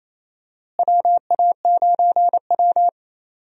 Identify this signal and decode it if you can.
Morse code, heard as a keyed tone, about 28 wpm, 700 Hz. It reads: WA9W
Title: USA Call Signs - 4 Characters - Repeated 3x - 28wpm